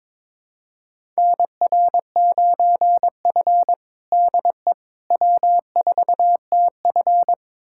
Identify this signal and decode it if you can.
Morse code, heard as a keyed tone, about 22 wpm, 700 Hz. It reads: NR9F DE W4TF
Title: USA Callsign DE Callsign - 4 Characters - 22wpm